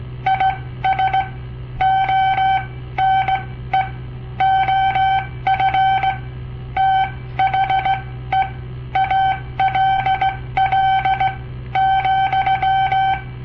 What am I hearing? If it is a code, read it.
Morse code, heard as a keyed tone, about 17 wpm, spaced as for 13 wpm, 770 Hz. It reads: IS ONE OF THE ALL,
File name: papertapes / local / Code Practice-01